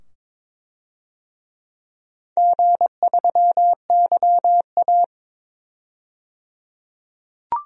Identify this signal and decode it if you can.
Morse code, heard as a keyed tone, about 22 wpm, 700 Hz. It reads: G3YA
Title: DXCC-100 Call Signs - 4 Characters - 22wpm